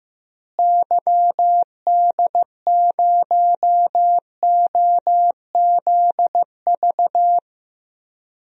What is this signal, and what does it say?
Morse code, heard as a keyed tone, about 15 wpm, 700 Hz. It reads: YD0OZV
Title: DXCC-100 Call Signs - 6 Characters - Repeated 2x - 15wpm